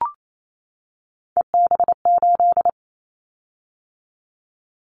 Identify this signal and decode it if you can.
Morse code, heard as a keyed tone, about 28 wpm, 700 Hz. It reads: EB8
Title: Callsign Prefixes for Top 50 DXCC - 28wpm